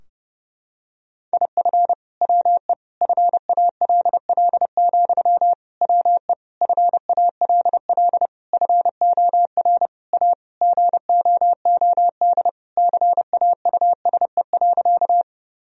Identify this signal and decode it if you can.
Morse code, heard as a keyed tone, about 30 wpm, 700 Hz. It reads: IF WE FALL, WE FALL FOR A GOOD CAUSE.
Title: Sentences from Top 1000 Words - Review Part 2 - 30wpm